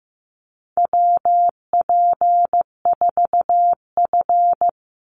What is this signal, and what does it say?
Morse code, heard as a keyed tone, about 15 wpm, 700 Hz. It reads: WP4F
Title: Call Signs Repeated 2x - 15wpm